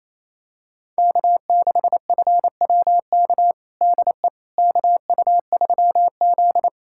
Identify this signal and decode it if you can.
Morse code, heard as a keyed tone, about 28 wpm, 700 Hz. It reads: K6FWK DE KU3Z
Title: USA Callsign DE Callsign 28wpm